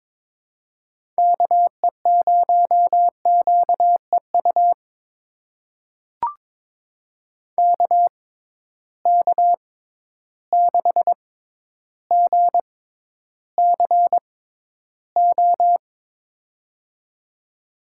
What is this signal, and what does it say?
Morse code, heard as a keyed tone, about 21 wpm, spaced as for 11 wpm, 700 Hz. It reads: KE0QEU KK6GCO
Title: USA Call Signs - 6 Characters - 6x Character Spacing - 22wpm